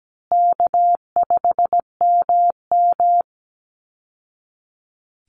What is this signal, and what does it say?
Morse code, heard as a keyed tone, about 17 wpm, 700 Hz. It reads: K5MM
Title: Easy Call Signs Repeated 3x - 17wpm